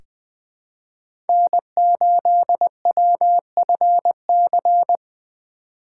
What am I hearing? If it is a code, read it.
Morse code, heard as a keyed tone, about 20 wpm, 700 Hz. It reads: N8WFC
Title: USA Call Signs - 5 Characters - 4x Character Spacing - 20wpm